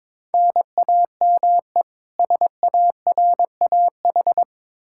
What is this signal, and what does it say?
Morse code, heard as a keyed tone, about 22 wpm, 700 Hz. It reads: NAME SARAH
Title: K1USN SST QSO Practice 22wpm